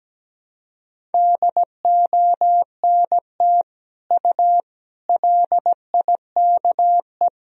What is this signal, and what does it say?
Morse code, heard as a keyed tone, about 17 wpm, 700 Hz. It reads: DONT U LIKE
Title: Sentences from Top 300 Words and Abbreviations 17wpm